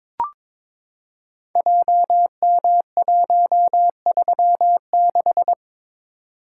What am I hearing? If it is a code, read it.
Morse code, heard as a keyed tone, about 22 wpm, 700 Hz. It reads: JM136
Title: Code Groups - 5 Letter:Numbers - 22wpm